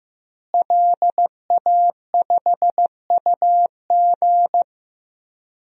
Morse code, heard as a keyed tone, about 15 wpm, 700 Hz. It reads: LA5UG
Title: DXCC-100 Call Signs - 5 Characters - Repeated 2x - 15wpm